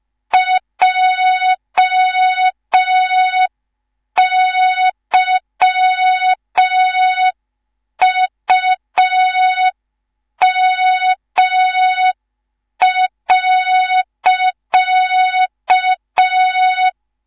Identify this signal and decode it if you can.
Morse code, heard as a keyed tone, about 5 wpm, 3700 Hz. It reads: JYUM.
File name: Morze-HA2MN / hang / 19 GYAKORLAT